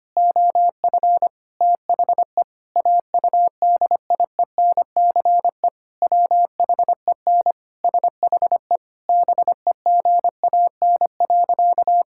Morse code, heard as a keyed tone, about 25 wpm, 700 Hz. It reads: OF THE AUDIENCE WHEN SHE BEGAN.
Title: Sentences from Top 1000 Words - Review Part 5 - 25wpm